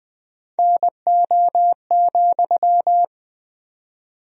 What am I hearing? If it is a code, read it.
Morse code, heard as a keyed tone, about 19 wpm, 700 Hz. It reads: NO,
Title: 4x Word-Spacing - Sentences from Top 300 Words - 20wpm